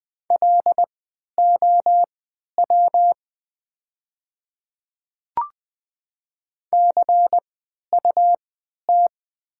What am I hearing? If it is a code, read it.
Morse code, heard as a keyed tone, about 20 wpm, spaced as for 11 wpm, 700 Hz. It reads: LOW CUT
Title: 3x Character Spacing - Top 300 Words - 20wpm